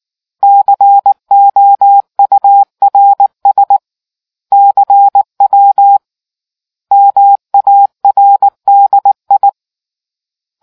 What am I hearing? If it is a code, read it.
Morse code, heard as a keyed tone, about 19 wpm, 800 Hz. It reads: COURS CW MARDI